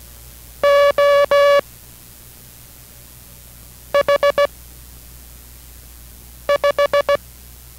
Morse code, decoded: O H 5

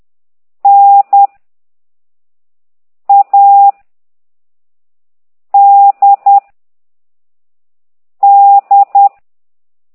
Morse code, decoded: N A D D

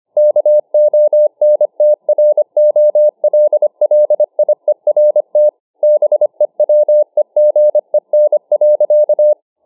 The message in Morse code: KONTROLLIERT BEWEGEN.

